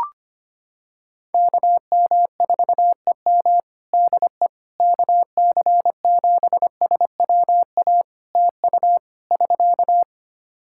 Morse code, heard as a keyed tone, about 25 wpm, 700 Hz. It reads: KM4EM DE KC7SWA TU <SK>
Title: Practice QSO - Exchange 4 - Ending - 25wpm